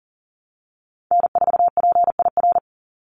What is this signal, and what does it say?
Morse code, heard as a keyed tone, about 40 wpm, 700 Hz. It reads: N4PIR
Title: Call Signs - 40wpm